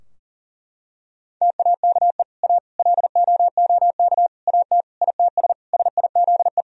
Morse code, heard as a keyed tone, about 40 wpm, 700 Hz. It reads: TAKE A LOOK AT ITS SIZE
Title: Sentences from Top 1000 Words - Review Part 2 - 40wpm